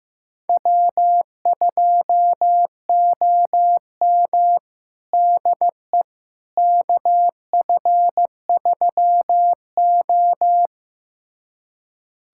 Morse code, heard as a keyed tone, about 15 wpm, 700 Hz. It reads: W2OM DE KF3O
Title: USA Callsign DE Callsign - 4 Characters - 15wpm